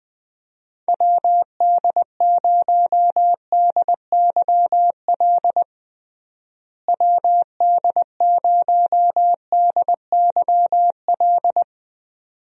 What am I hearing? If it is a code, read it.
Morse code, heard as a keyed tone, about 20 wpm, 700 Hz. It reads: WD0DYL WD0DYL
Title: USA Call Signs - 6 Characters - Repeated 2x - 20wpm